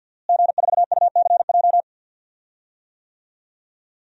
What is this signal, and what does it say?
Morse code, heard as a keyed tone, about 45 wpm, 700 Hz. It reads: G4UCJ
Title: DXCC-100 Call Signs - 5 Characters - Repeated 2x - 50wpm